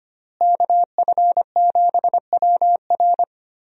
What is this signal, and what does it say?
Morse code, heard as a keyed tone, about 25 wpm, 700 Hz. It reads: KF7WR
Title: USA Call Signs - 5 Characters - Repeated 3x - 25wpm